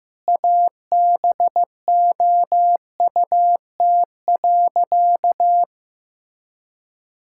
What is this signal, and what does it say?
Morse code, heard as a keyed tone, about 15 wpm, 700 Hz. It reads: ABOUT.
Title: Sentences from Top 1600 Words 15wpm